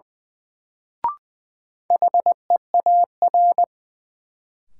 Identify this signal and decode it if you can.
Morse code, heard as a keyed tone, about 20 wpm, 700 Hz. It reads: HEAR
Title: Sentences from Top 300 Words - New Word Drill - 20wpm